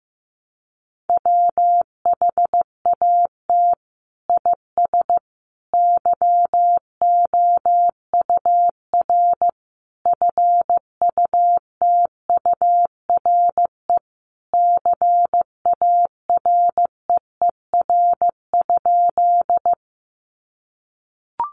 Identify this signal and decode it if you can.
Morse code, heard as a keyed tone, about 15 wpm, 700 Hz. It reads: WHAT IS YOUR FUTURE CAREER?
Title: Sentences from Top 1000 Words 15wpm